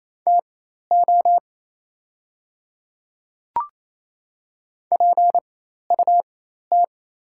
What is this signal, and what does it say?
Morse code, heard as a keyed tone, about 28 wpm, spaced as for 13 wpm, 690 Hz. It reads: TO PUT
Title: 4x Character Spacing - Top 200 Words - 28wpm